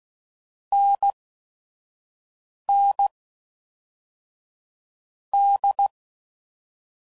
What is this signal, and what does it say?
Morse code, heard as a keyed tone, about 16 wpm, spaced as for 4 wpm, 800 Hz. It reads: NND
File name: Combinations / Difficult Combos / NDB NDB6